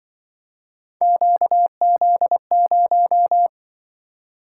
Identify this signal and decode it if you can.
Morse code, heard as a keyed tone, about 24 wpm, 700 Hz. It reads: QZ0
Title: Speed-Racing - Three Letter-Number Combinations - 20wpm